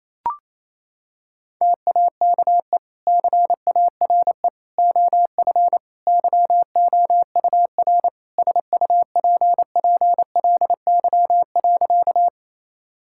Morse code, heard as a keyed tone, about 27 wpm, 700 Hz. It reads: TAKE CARE OF YOUR SUPPLY.
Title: Sentences from Top 1000 Words - Review Part 5 - 28wpm